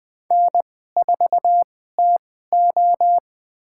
Morse code, heard as a keyed tone, about 20 wpm, spaced as for 14 wpm, 700 Hz. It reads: N4TO
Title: USA Call Signs - 2x Character Spacing - 20wpm